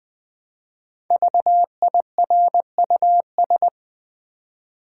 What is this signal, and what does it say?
Morse code, heard as a keyed tone, about 20 wpm, 700 Hz. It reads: VIRUS
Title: Build A Word - 1 Letter At A Time - 5-Letter Words - 20wpm